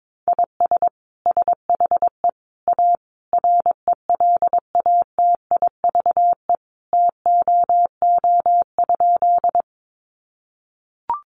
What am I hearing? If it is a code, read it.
Morse code, heard as a keyed tone, about 22 wpm, 700 Hz. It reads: IS SHE A RELATIVE TOO?
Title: Sentences from Top 1400 Words 22wpm